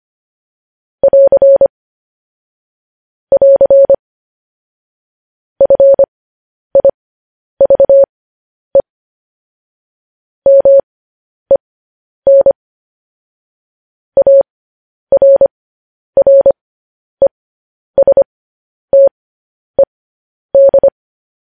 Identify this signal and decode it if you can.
Morse code, heard as a keyed tone, about 25 wpm, spaced as for 10 wpm, 550 Hz. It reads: <AR> <AR> FIVE MEN ARRESTED
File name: podcast / News headlines-1x-25x10